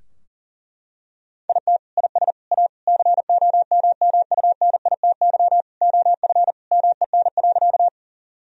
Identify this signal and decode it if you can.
Morse code, heard as a keyed tone, about 40 wpm, 700 Hz. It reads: IT IS A COMMUNITY OF MEN.
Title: Sentences from Top 1000 Words - Review Part 4 - 40wpm